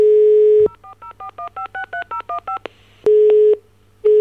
DTMF, heard as telephone touch tones, tones of about 110 ms. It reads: 04041563015